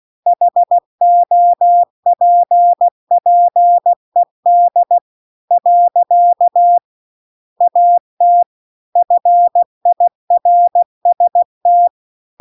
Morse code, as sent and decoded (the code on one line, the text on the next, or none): .... --- .--. .--. -.. / .-.-.- / .- - / ..-. .. .-. ... -
HOPPED . AT FIRST